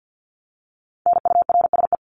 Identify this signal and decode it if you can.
Morse code, heard as a keyed tone, about 50 wpm, 700 Hz. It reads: NURSE
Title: Top 1751-2000 Words 50wpm